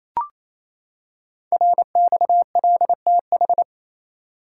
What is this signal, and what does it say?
Morse code, heard as a keyed tone, about 27 wpm, 700 Hz. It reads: RXLTH
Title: Code Groups - 5 Letters - 28wpm